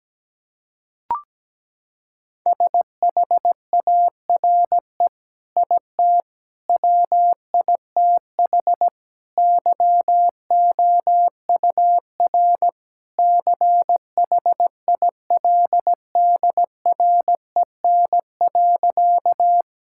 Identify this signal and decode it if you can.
Morse code, heard as a keyed tone, about 17 wpm, 700 Hz. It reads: SHARE IT WITH YOUR CHILDREN.
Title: Sentences from Top 1000 Words - Review Part 6 - 17wpm